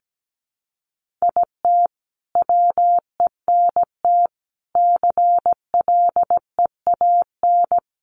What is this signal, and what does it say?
Morse code, heard as a keyed tone, about 17 wpm, 700 Hz. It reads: IT WENT CLEAN
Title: Sentences from Top 1200 Words 17wpm